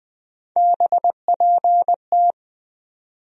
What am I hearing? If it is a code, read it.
Morse code, heard as a keyed tone, about 20 wpm, 700 Hz. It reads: BPT
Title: USA Call Sign Postfixes - 3 Letters 20wpm